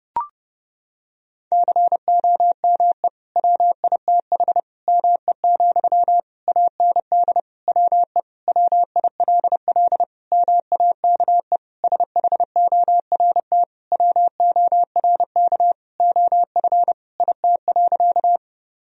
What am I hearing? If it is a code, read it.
Morse code, heard as a keyed tone, about 30 wpm, 700 Hz. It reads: COME WITH ME, AND WE WILL MAKE SHORT WORK OF IT.
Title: Sentences from Top 1000 Words - Review Part 2 - 30wpm